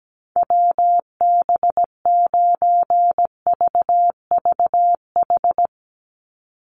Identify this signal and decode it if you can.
Morse code, heard as a keyed tone, about 17 wpm, 700 Hz. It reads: WB9VVH
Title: USA Call Signs 17wpm